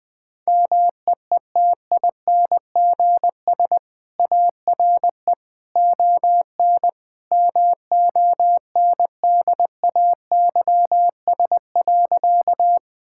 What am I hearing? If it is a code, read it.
Morse code, heard as a keyed tone, about 21 wpm, 700 Hz. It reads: MEETINGS ARE ON MONDAYS.
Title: Easy Sentences with Hints - 20wpm